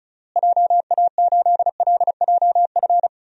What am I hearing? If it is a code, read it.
Morse code, heard as a keyed tone, about 36 wpm, 700 Hz. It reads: JA8LJF
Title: DXCC-100 Call Signs - 6 Characters - Repeated 3x - 35wpm